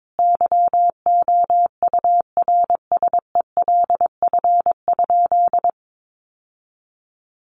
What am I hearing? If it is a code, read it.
Morse code, heard as a keyed tone, about 22 wpm, 700 Hz. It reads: YOURSELF?
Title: Sentences from Top 1200 Words 22wpm